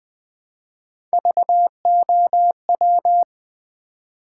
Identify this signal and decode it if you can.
Morse code, heard as a keyed tone, about 20 wpm, 700 Hz. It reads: VOW
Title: Build A Word - 1 Letter At A Time - 3 to 5-Letter Words - 20wpm